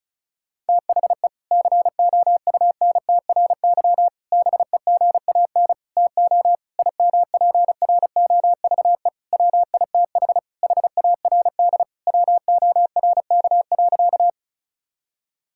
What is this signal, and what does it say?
Morse code, heard as a keyed tone, about 36 wpm, 700 Hz. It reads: THE COUNTRY BEGAN TO IMPROVE WITH HARD WORK.
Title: Sentences from Top 1000 Words - Review Part 5 - 35wpm